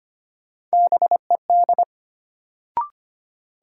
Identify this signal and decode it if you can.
Morse code, heard as a keyed tone, about 25 wpm, 700 Hz. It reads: BED
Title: Sentences from Top 500 Words - New Word Drill - 25wpm